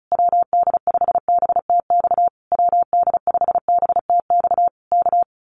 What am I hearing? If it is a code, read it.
Morse code, heard as a keyed tone, about 34 wpm, 700 Hz. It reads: WD5BTX WD5BTX K
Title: Common QSO Phrases Encore 35wpm